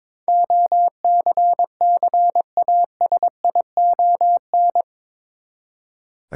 Morse code, heard as a keyed tone, about 22 wpm, 700 Hz. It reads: OCCASION
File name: practice / 22wpm / Eight Letter Words - 22wpm